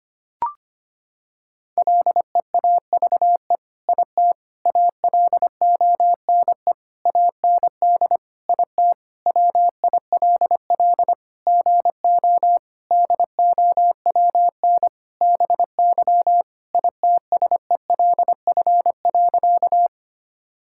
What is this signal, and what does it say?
Morse code, heard as a keyed tone, about 25 wpm, 700 Hz. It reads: LEAVE IT ALONE AND IT WILL GO DOWN BY ITSELF.
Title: Sentences from Top 1000 Words - Review Part 5 - 25wpm